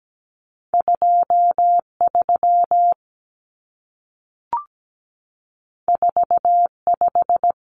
Morse code, heard as a keyed tone, about 17 wpm, 700 Hz. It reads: 23 45
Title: Numbers - 2-Digit 17wpm